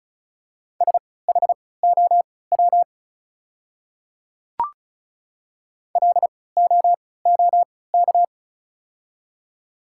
Morse code, heard as a keyed tone, about 34 wpm, spaced as for 20 wpm, 700 Hz. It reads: SHOW LOOK
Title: 3x Character Spacing - Top 300 Words - 35wpm